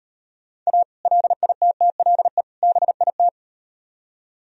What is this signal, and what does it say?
Morse code, heard as a keyed tone, about 38 wpm, 700 Hz. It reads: A LITTLE BIT